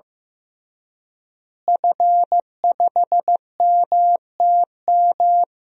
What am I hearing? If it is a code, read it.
Morse code, heard as a keyed tone, about 15 wpm, 700 Hz. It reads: F5MTM
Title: DXCC-100 Call Signs - 5 Characters - Repeated 3x - 15wpm